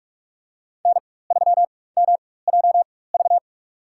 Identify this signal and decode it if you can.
Morse code, heard as a keyed tone, about 43 wpm, spaced as for 21 wpm, 700 Hz. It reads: N3MJV
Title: USA Call Signs - 5 Characters - 4x Character Spacing - 45wpm